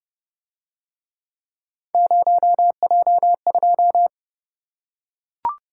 Morse code, heard as a keyed tone, about 29 wpm, 700 Hz. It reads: 0J2